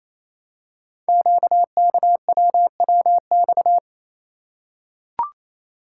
Morse code, heard as a keyed tone, about 28 wpm, 700 Hz. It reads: QKWWX
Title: Code Groups - 5 Letters - 28wpm